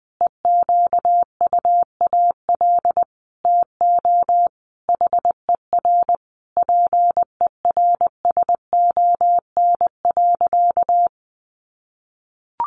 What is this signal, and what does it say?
Morse code, heard as a keyed tone, about 20 wpm, 700 Hz. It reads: EQUAL TO HER PERSON.
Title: Sentences from Top 1400 Words 20wpm